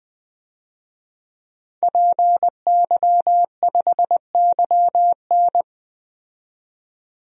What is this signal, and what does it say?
Morse code, heard as a keyed tone, about 20 wpm, 700 Hz. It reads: PY5YN
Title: DXCC-100 Call Signs - 5 Characters - 20wpm